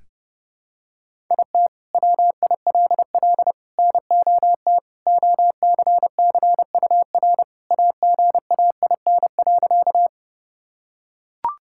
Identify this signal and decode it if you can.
Morse code, heard as a keyed tone, about 30 wpm, 700 Hz. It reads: IT WILL NOT OCCUR AGAIN.
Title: Sentences from Top 1000 Words - Review Part 6 - 30wpm